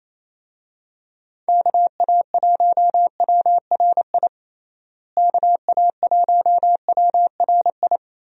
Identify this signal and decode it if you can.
Morse code, heard as a keyed tone, about 28 wpm, 700 Hz. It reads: KA1WRI KA1WRI
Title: USA Call Signs - 6 Characters - Repeated 2x - 28wpm